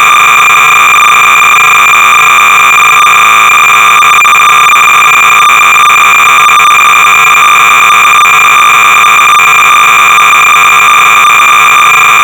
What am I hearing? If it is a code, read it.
Cassette data (Kansas City type).
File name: sorcerer / 300baud / Killer-Gorilla-(1983)(Dubois-and-McNamara)(Displayed Video)-Sorcerer-300baud